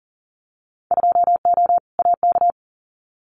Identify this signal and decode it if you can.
Morse code, heard as a keyed tone, about 40 wpm, 700 Hz. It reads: 2O AK
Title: Winter Field Day Exchanges Encore 40wpm